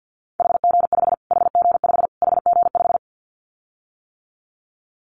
Morse code, heard as a keyed tone, about 50 wpm, 700 Hz. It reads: HG5 HG5 HG5
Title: Callsign Prefixes for Top 25 DXCC - Repeated 3x - 50wpm